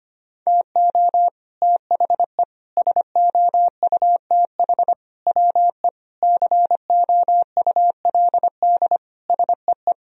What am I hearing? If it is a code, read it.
Morse code, heard as a keyed tone, about 25 wpm, 700 Hz. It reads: TO THE SOUTH WE COULD SEE
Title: Sentences from Top 1000 Words - Review Part 2 - 25wpm